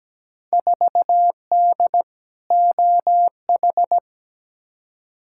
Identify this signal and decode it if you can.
Morse code, heard as a keyed tone, about 17 wpm, 700 Hz. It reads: 4D OH
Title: Field Day Exchanges Encore - 17wpm